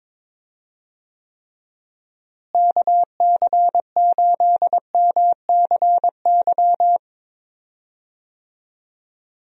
Morse code, heard as a keyed tone, about 22 wpm, 700 Hz. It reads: KC8MCY
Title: USA Call Signs - 6 Characters - Repeated 3x - 22wpm